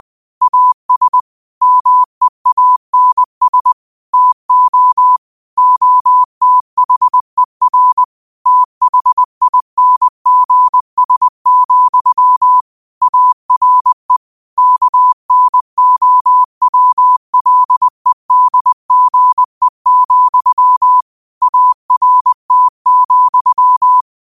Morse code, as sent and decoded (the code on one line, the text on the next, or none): .- ... / -- . .- -. ... / - --- / --- - .... .-. / - .... .. -. --. ... --..-- / .- .-. / -.- -. --- .-- .-.. -.. --. --..-- / .- .-. - --..--
AS MEANS TO OTHER THINGS, ARE KNOWLEDGE, ART,